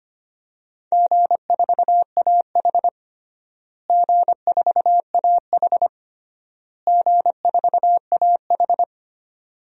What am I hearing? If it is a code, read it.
Morse code, heard as a keyed tone, about 25 wpm, 700 Hz. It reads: G4AH G4AH G4AH
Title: DXCC-100 Call Signs - 4 Characters - Repeated 3x - 25wpm